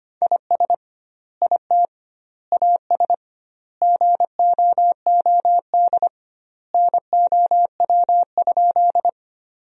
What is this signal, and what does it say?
Morse code, heard as a keyed tone, about 25 wpm, 700 Hz. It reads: IS IT AS GOOD NOW?
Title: 2x Word-Spacing - Sentences from Top 200 Words - 25wpm